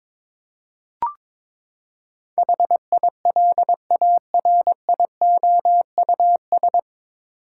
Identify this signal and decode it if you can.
Morse code, heard as a keyed tone, about 22 wpm, 700 Hz. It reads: HILARIOUS
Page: Morse Code Ninja practice 22wpm